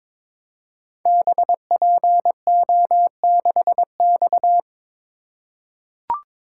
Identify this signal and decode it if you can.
Morse code, heard as a keyed tone, about 23 wpm, 700 Hz. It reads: BPO6X
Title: Code Groups - 5 Letter:Numbers - 22wpm